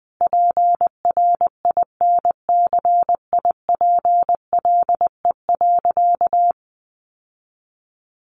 Morse code, heard as a keyed tone, about 20 wpm, 700 Hz. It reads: PRINCIPLE.
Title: Sentences from Top 1400 Words 20wpm